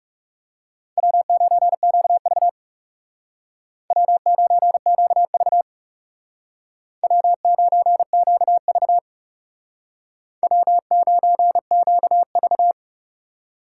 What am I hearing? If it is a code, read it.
Morse code, heard as a keyed tone, about 36 wpm, 700 Hz. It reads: W9QV W9QV W9QV W9QV
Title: Speed Racing - USA 1x2 Call Signs 30wpm